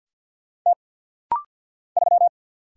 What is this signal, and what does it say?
Morse code, heard as a keyed tone, about 49 wpm, 690 Hz. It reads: T 3
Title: Single Letter-Number - Rapid-Fire 50wpm